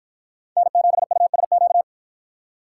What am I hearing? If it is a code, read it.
Morse code, heard as a keyed tone, about 53 wpm, 700 Hz. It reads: N7USQ